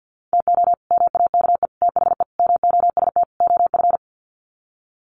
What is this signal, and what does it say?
Morse code, heard as a keyed tone, about 50 wpm, 700 Hz. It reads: TO MAKE THE MOST OF
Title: Sets of 5 Words Encore2 50wpm